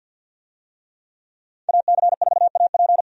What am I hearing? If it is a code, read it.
Morse code, heard as a keyed tone, about 51 wpm, 700 Hz. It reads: AC4AP
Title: USA Call Signs - 5 Characters - 50wpm